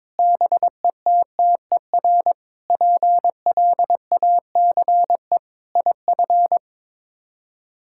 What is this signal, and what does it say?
Morse code, heard as a keyed tone, about 22 wpm, 700 Hz. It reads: BETTER PLACE IF
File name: practice / 22wpm / Sets of 8 Words - 22wpm